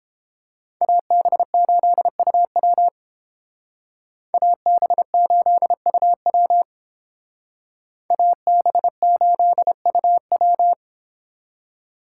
Morse code, heard as a keyed tone, about 30 wpm, 700 Hz. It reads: AB8UW AB8UW AB8UW